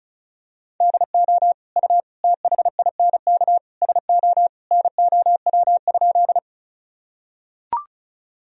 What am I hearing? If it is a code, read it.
Morse code, heard as a keyed tone, about 36 wpm, 700 Hz. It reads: DO U THINK SO NOW?
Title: Sentences from Top 200 Words and Abbreviations 35wpm